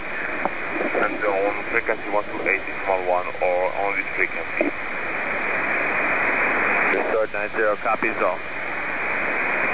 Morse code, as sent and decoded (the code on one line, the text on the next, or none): none